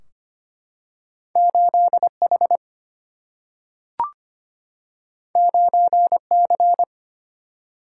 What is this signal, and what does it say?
Morse code, heard as a keyed tone, about 25 wpm, 700 Hz. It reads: 8 H 9 C
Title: Single-Letter Single-Number 25wpm